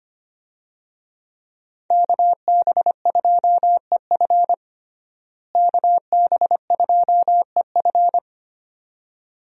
Morse code, heard as a keyed tone, about 25 wpm, 700 Hz. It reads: KB2EF KB2EF
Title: USA Call Signs - 5 Characters - Repeated 2x - 25wpm